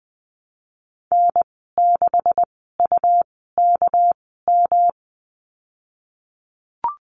Morse code, heard as a keyed tone, about 20 wpm, spaced as for 15 wpm, 700 Hz. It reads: N6UKM
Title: 200 Lesson 32g - Letter k - Call Signs 20wpm